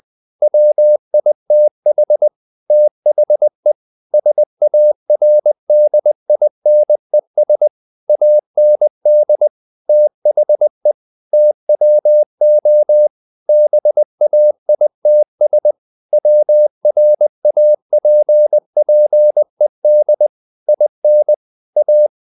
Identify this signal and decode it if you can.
Morse code, heard as a keyed tone, about 20 wpm, 600 Hz. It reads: WITH THE SARDINES AND THE TWO BAITS WRAPPED IN A
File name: the old man and the sea / the old man and the sea part 1 20wpm